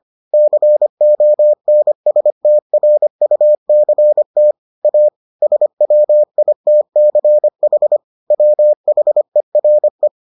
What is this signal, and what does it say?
Morse code, heard as a keyed tone, about 25 wpm, 600 Hz. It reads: CONSTRUCT A SWITCH WHERE